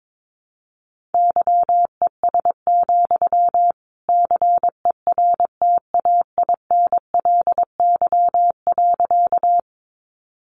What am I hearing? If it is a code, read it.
Morse code, heard as a keyed tone, about 22 wpm, 700 Hz. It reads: YES, CERTAINLY.
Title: Sentences from Top 900 Words 22wpm